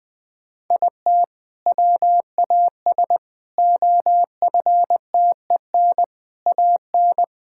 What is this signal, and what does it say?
Morse code, heard as a keyed tone, about 20 wpm, 700 Hz. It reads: IT WAS OFTEN AN
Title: Sentences from Top 1000 Words - Review Part 6 - 20wpm